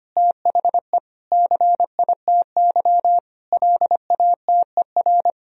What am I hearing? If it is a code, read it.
Morse code, heard as a keyed tone, about 25 wpm, 700 Hz. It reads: THE CITY LATER
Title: Sentences from Top 1000 Words - Review Part 2 - 25wpm